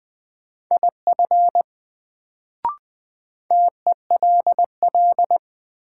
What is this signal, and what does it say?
Morse code, heard as a keyed tone, about 20 wpm, 700 Hz. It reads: IF TELL
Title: Sets of 2 Words - New Word Drill - 20wpm